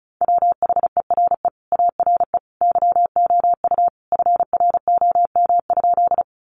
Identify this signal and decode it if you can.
Morse code, heard as a keyed tone, about 34 wpm, 700 Hz. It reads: WHERE ARE YOU FROM?
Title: Sentences from Top 200 Words 35wpm